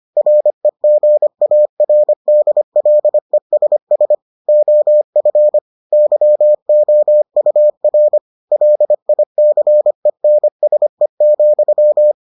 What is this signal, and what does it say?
Morse code, heard as a keyed tone, about 25 wpm, 600 Hz. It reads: REGARDLESS OF YOUR LICENSE,